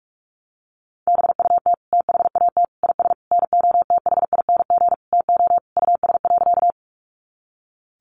Morse code, heard as a keyed tone, about 44 wpm, 700 Hz. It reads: BUT THAT IS NOTHING TO US.
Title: Sentences from Top 500 Words 45wpm